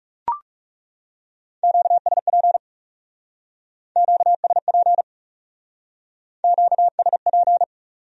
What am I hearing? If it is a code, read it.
Morse code, heard as a keyed tone, about 38 wpm, 700 Hz. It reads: QSP QSP QSP